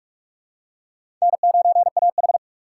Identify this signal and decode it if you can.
Morse code, heard as a keyed tone, about 44 wpm, 700 Hz. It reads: N9AH
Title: USA Call Signs - 4 Characters - 6x Character Spacing - 45wpm